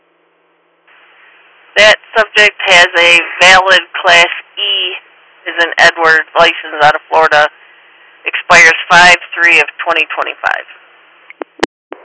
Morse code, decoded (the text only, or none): none